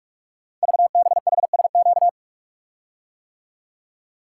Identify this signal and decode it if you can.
Morse code, heard as a keyed tone, about 44 wpm, 700 Hz. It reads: VBHSQ
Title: Code Groups - 5 Letter:Numbers - 45wpm